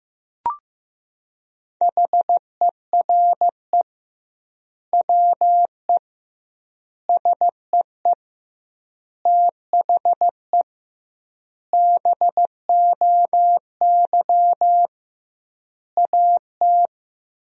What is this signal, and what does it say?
Morse code, heard as a keyed tone, about 15 wpm, 700 Hz. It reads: HERE WE SEE THE BOY AT
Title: 2x Word-Spacing - Sentences from Top 300 Words - 15wpm